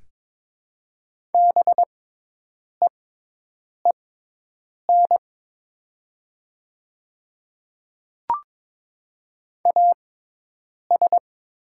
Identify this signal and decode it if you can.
Morse code, heard as a keyed tone, about 22 wpm, spaced as for 8 wpm, 700 Hz. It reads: BEEN AS
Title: 6x Character Spacing - Top 100 Words - 22wpm